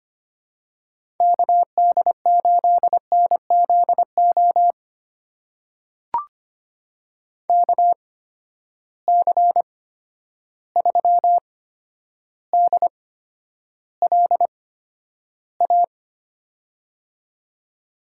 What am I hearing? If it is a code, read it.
Morse code, heard as a keyed tone, about 25 wpm, spaced as for 11 wpm, 700 Hz. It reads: KD8NZO KC3DLA
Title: USA Call Signs - 6 Characters - 8x Character Spacing - 25wpm